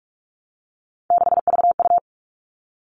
Morse code, heard as a keyed tone, about 44 wpm, 700 Hz. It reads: 6VU
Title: All Cut Numbers - 3-Digit 45wpm